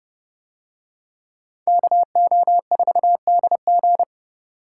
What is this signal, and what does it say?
Morse code, heard as a keyed tone, about 28 wpm, 700 Hz. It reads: KO4DG